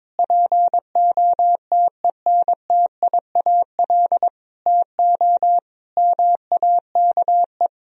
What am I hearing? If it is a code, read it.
Morse code, heard as a keyed tone, about 22 wpm, 700 Hz. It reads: POTENTIAL TO MAKE
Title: Sentences from Top 1000 Words - Review Part 4 - 22wpm